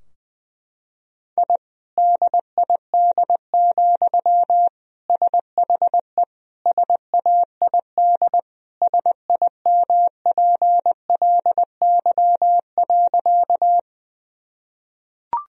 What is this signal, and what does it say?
Morse code, heard as a keyed tone, about 20 wpm, 700 Hz. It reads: I DID, SHE SAID SIMPLY.
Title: Sentences from Top 1000 Words - Review Part 4 - 20wpm